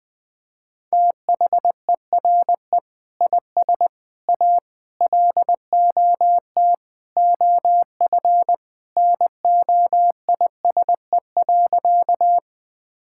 Morse code, heard as a keyed tone, about 20 wpm, 700 Hz. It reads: THERE IS A LOT OF NOISE.
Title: Sentences with 2-Word Phrases - 20wpm